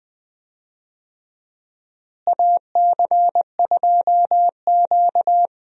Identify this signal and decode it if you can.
Morse code, heard as a keyed tone, about 20 wpm, 700 Hz. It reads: AC2Q
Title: USA Call Signs - 4 Characters - Repeated 2x - 20wpm